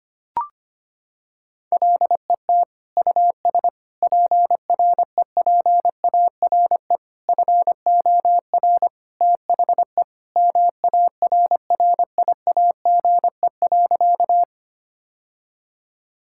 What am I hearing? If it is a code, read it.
Morse code, heard as a keyed tone, about 25 wpm, 700 Hz. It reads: LET US PREPARE FOR THE MARRIAGE.
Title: Sentences from Top 1000 Words - Review Part 6 - 25wpm